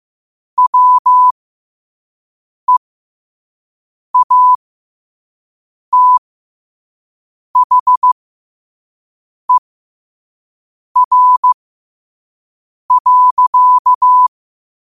Morse code, heard as a keyed tone, about 15 wpm, spaced as for 5 wpm, 1000 Hz. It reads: WEATHER.
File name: qotdfiles / 5wpm